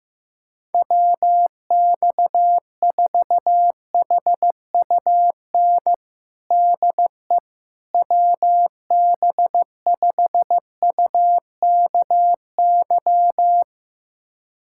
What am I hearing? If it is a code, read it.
Morse code, heard as a keyed tone, about 15 wpm, 700 Hz. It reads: WX4HUN DE WB5UKY
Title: USA Callsign DE Callsign - 6 Characters - 15wpm